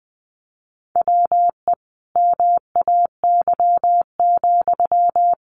Morse code, heard as a keyed tone, about 20 wpm, 700 Hz. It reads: WE MAY,
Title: Sentences from Top 1500 Words 20wpm